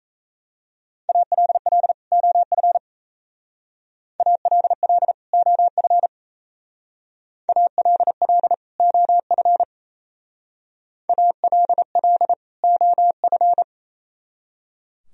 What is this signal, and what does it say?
Morse code, heard as a keyed tone, about 34 wpm, 700 Hz. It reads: ALL OF ALL OF ALL OF ALL OF
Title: Speed-Racing - Sets of 2 Words 28wpm